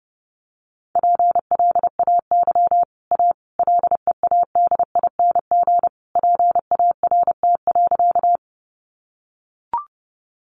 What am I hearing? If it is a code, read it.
Morse code, heard as a keyed tone, about 30 wpm, 700 Hz. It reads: PLAY A LEADING PART.
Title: Sentences from Top 1100 Words 30wpm